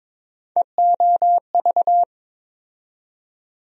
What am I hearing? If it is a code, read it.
Morse code, heard as a keyed tone, about 21 wpm, 700 Hz. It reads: EOV
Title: USA Call Sign Postfixes - 3 Letters 22wpm